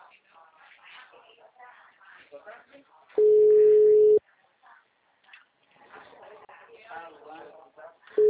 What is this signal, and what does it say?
Ringback tone (Europe, 425 Hz), with speech.
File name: out- -2018